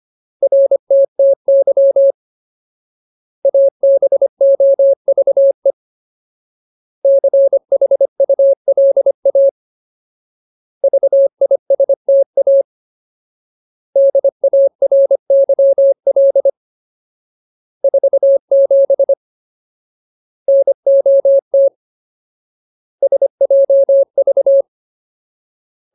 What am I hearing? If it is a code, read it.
Morse code, heard as a keyed tone, about 25 wpm, 550 Hz. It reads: RTTY ABOVE CHULA VISTA DARYL 47 NOT SJV
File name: Ultimate Word List / 25WPM Ultimate